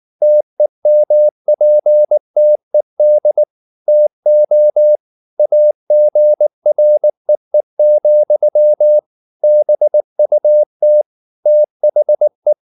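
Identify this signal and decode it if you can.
Morse code, heard as a keyed tone, about 19 wpm, 600 Hz. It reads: TEMPTED TO AGREE, BUT THE
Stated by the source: Morse code at 19 wpm from CW med Gnister, reading Ken Follett's novel Never in English.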